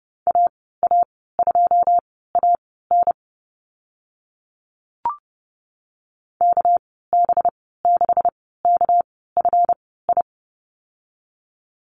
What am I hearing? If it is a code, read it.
Morse code, heard as a keyed tone, about 30 wpm, spaced as for 17 wpm, 700 Hz. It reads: AA2AN KB6KFI
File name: Lesson 37g - Letter x - Call Signs 30wpm